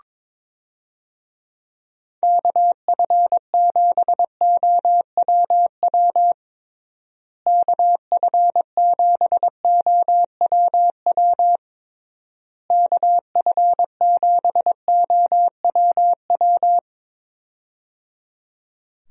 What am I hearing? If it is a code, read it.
Morse code, heard as a keyed tone, about 22 wpm, 700 Hz. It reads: KF7OWW KF7OWW KF7OWW
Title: USA Call Signs - 6 Characters - Repeated 3x - 22wpm